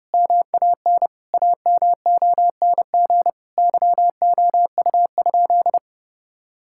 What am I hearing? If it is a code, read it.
Morse code, heard as a keyed tone, about 30 wpm, 700 Hz. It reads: MAN AMONG YOU?
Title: Sentences from Top 1600 Words 30wpm